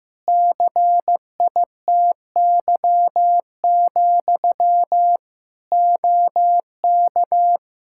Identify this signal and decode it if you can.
Morse code, heard as a keyed tone, about 15 wpm, 700 Hz. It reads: CITY, OK
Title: Practice QSO - Exchange 1 - Beginning - 15wpm